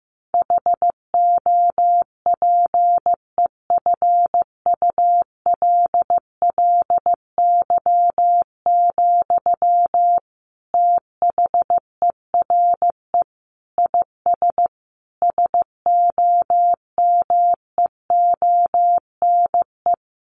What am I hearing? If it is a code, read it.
Morse code, heard as a keyed tone, about 15 wpm, 700 Hz. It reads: HOPEFULLY, THERE IS SOMEONE
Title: Sentences from Top 1500 Words 15wpm